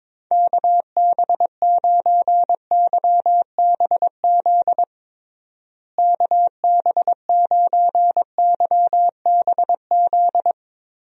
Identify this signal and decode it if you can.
Morse code, heard as a keyed tone, about 22 wpm, 700 Hz. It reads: KB9YBZ KB9YBZ
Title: USA Call Signs - 6 Characters - Repeated 3x - 22wpm